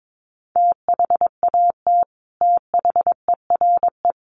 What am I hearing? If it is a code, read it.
Morse code, heard as a keyed tone, about 22 wpm, 700 Hz. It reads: THAT THERE
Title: Sets of 5 Words Encore2 22wpm